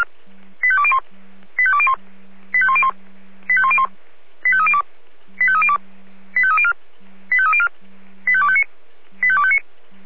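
Five-tone selective call (ZVEI), heard as quick runs of five tones, tones of about 70 ms. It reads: 85201 85201 85101 85302 85302 85304 85304 85259 85259
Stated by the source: ZVEI five-tone selective call 85102 received on an SDR.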